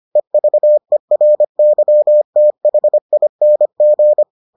Morse code, heard as a keyed tone, about 25 wpm, 600 Hz. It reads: EVERYTHING